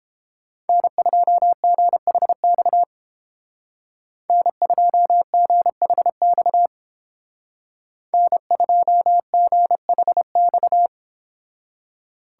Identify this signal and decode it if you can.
Morse code, heard as a keyed tone, about 31 wpm, 700 Hz. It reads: N2GHX N2GHX N2GHX